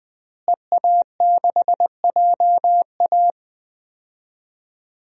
Morse code, heard as a keyed tone, about 20 wpm, 700 Hz. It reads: EA6JA
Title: DXCC-100 Call Signs - 5 Characters - Repeated 2x - 20wpm